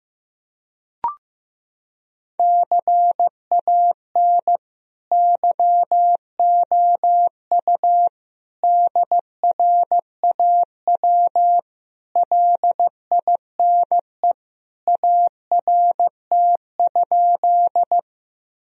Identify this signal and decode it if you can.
Morse code, heard as a keyed tone, about 15 wpm, 700 Hz. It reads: CAN YOU DRAW LINE ART?
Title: Sentences from Top 1000 Words - Review Part 3 - 15wpm